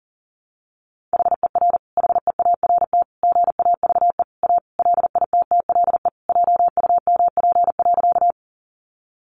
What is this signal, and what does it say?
Morse code, heard as a keyed tone, about 40 wpm, 700 Hz. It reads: HER HEART GAVE A LITTLE JUMP.
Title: Sentences from Top 1300 Words 40wpm